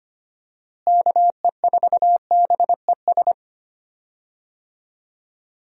Morse code, heard as a keyed tone, about 25 wpm, 700 Hz. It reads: KE4BES
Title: USA Call Signs - 6 Characters - Repeated 2x - 25wpm